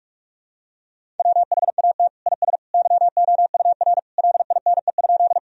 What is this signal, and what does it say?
Morse code, heard as a keyed tone, about 42 wpm, 700 Hz. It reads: WHAT IS YOUR LINE?
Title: Sentences from Top 1000 Words - Review Part 1 - 45wpm